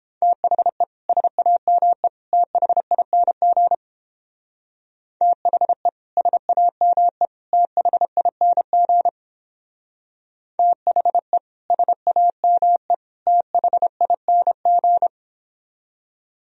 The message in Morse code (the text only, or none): THE SAME THING THE SAME THING THE SAME THING